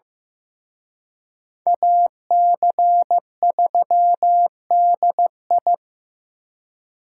Morse code, heard as a keyed tone, about 18 wpm, 700 Hz. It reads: AC3DI
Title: Speed Racing - USA 2x2 Call Signs 15wpm